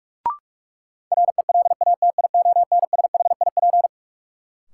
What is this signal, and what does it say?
Morse code, heard as a keyed tone, about 45 wpm, 700 Hz. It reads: RELATIONSHIP
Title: Sentences from Top 1100 Words - New Word Drill Alt - 45wpm